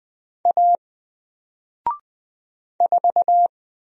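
Morse code, heard as a keyed tone, about 20 wpm, 700 Hz. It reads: A 4
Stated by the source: Single letter and number Morse drill at 20 wpm, from N4LCD.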